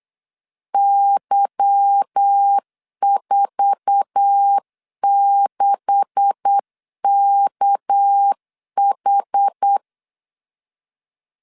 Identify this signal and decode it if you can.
Morse code, heard as a keyed tone, about 8 wpm, 800 Hz. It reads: Y46KH
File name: CW / DET / 1 9 1500C S A 8wpm 5637